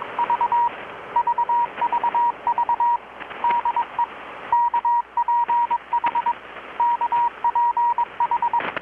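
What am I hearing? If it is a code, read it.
Morse code, heard as a keyed tone, about 22 wpm, 950 Hz, with radio noise. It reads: V VVV DE KPH KPH
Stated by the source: Morse from KPH's Night of Nights XX maritime radiotelegraphy event, received in Chiapas, Mexico about 3,800 km away on a Kenwood R-600 with a grounded 10 m long-wire antenna.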